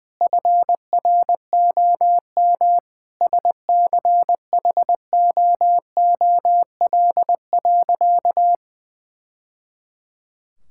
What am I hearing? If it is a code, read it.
Morse code, heard as a keyed tone, about 20 wpm, 700 Hz. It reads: FROM SCHOOL.
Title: Sentences from Top 1000 Words - Review Part 6 - 20wpm